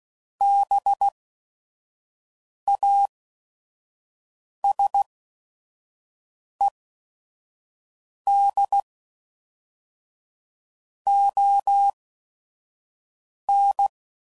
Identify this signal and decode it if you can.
Morse code, heard as a keyed tone, about 16 wpm, spaced as for 5 wpm, 800 Hz. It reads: BASED ON
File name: CW Code Groups @5WPM / Technical Subjects / Reactance